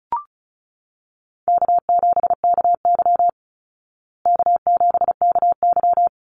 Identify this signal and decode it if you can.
Morse code, heard as a keyed tone, about 34 wpm, 700 Hz. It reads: K7KY K7KY
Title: Call Signs Repeated 3x - 35wpm